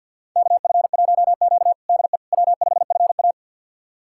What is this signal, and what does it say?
Morse code, heard as a keyed tone, about 49 wpm, 700 Hz. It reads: KF1Q DE W5FA
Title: USA Callsign DE Callsign - 4 Characters - 50wpm